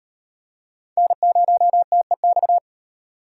Morse code, heard as a keyed tone, about 36 wpm, 700 Hz. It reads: N0TEX